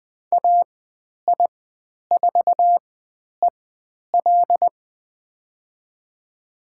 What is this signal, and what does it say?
Morse code, heard as a keyed tone, about 20 wpm, spaced as for 10 wpm, 700 Hz. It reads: AI4EL
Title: 109 Lesson 18g - Letter m - Call Signs 20wpm